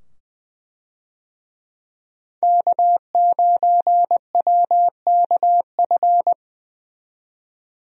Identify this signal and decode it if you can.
Morse code, heard as a keyed tone, about 20 wpm, 700 Hz. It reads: K9WKF
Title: USA Call Signs - 5 Characters - 20wpm